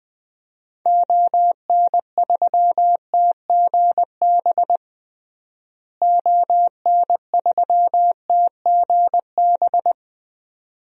Morse code, heard as a keyed tone, about 20 wpm, 700 Hz. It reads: ON3TGB ON3TGB
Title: DXCC-100 Call Signs - 6 Characters - Repeated 2x - 20wpm